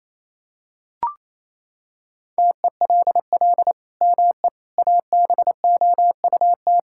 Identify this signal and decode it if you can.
Morse code, heard as a keyed tone, about 28 wpm, 700 Hz. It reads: TELL ME ABOUT
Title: Sentences from Top 1000 Words - Review Part 5 - 28wpm